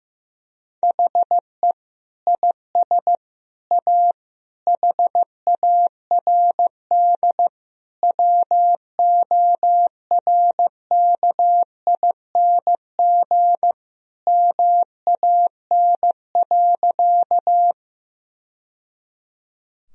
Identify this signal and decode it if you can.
Morse code, heard as a keyed tone, about 15 wpm, 700 Hz. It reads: HE IS A HARD WORKING MAN.
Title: Sentences from Top 1000 Words - Review Part 4 - 15wpm